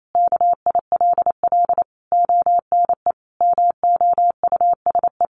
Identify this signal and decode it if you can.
Morse code, heard as a keyed tone, about 28 wpm, 700 Hz. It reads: KILL ONE MOUSE
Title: A Sound Of Thunder 28wpm